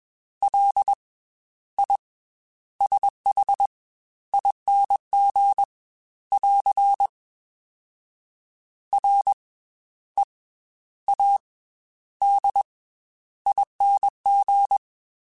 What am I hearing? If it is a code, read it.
Morse code, heard as a keyed tone, about 20 wpm, 800 Hz. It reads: L I S H ING <AR> R E A D ING